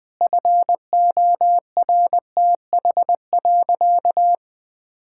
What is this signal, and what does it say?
Morse code, heard as a keyed tone, about 20 wpm, 700 Hz. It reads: FORTH.
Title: Sentences with 6-Word Phrases - 20wpm